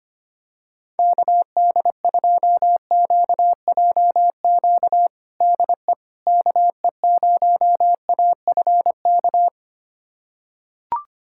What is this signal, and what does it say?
Morse code, heard as a keyed tone, about 25 wpm, 700 Hz. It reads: KD2QJQ DE KE0AFK
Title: USA Callsign DE Callsign 25wpm